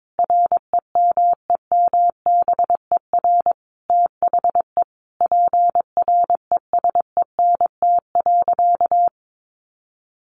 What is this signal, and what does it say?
Morse code, heard as a keyed tone, about 22 wpm, 700 Hz. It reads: REMEMBER THE PRESENT.
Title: Sentences from Top 800 Words 22wpm